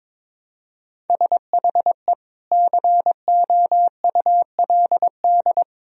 Morse code, heard as a keyed tone, about 22 wpm, 700 Hz. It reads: SHE COULD
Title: Sentences from Top 1000 Words - Review Part 2 - 22wpm